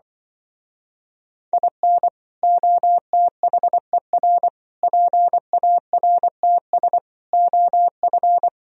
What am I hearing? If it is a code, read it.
Morse code, heard as a keyed tone, about 25 wpm, 700 Hz. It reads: IN OTHER PARTS OF
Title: Speed-Racing - Sets of 5 Words 20wpm